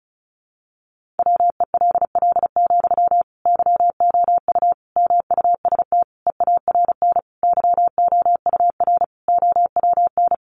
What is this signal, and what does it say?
Morse code, heard as a keyed tone, about 34 wpm, 700 Hz. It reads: WELL, YOU MUST EARN YOUR OWN
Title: Sentences from Top 1500 Words 35wpm